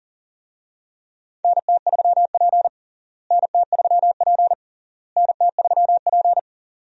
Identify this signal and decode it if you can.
Morse code, heard as a keyed tone, about 40 wpm, 700 Hz. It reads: NT3P NT3P NT3P
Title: USA Call Signs - 4 Characters - Repeated 3x - 40wpm